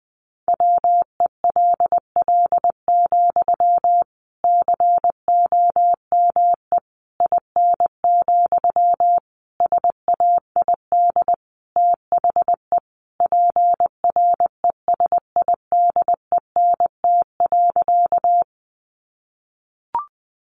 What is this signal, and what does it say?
Morse code, heard as a keyed tone, about 20 wpm, 700 Hz. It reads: WELL, COME IN, SAID THE PRESIDENT.
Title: Sentences from Top 1000 Words 20wpm